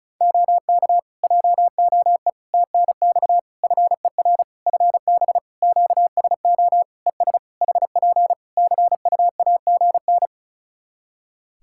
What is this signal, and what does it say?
Morse code, heard as a keyed tone, about 36 wpm, 700 Hz. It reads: OK JOE TNX FER FB QSO ES HP CUAGN
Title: Practice QSO - Exchange 4 - Ending - 35wpm